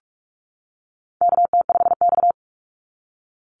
Morse code, heard as a keyed tone, about 46 wpm, 700 Hz. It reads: KT5X
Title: NA SOTA Chaser Callsigns 45wpm